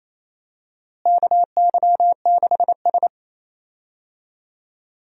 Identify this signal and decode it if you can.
Morse code, heard as a keyed tone, about 27 wpm, 700 Hz. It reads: KY6S
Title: USA Call Signs - 4 Characters - Repeated 3x - 28wpm